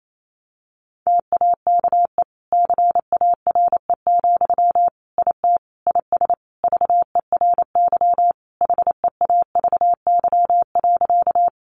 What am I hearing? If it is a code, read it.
Morse code, heard as a keyed tone, about 28 wpm, 700 Hz. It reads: TAKE CARE, IT IS VERY HEAVY.
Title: Sentences from Top 1100 Words 28wpm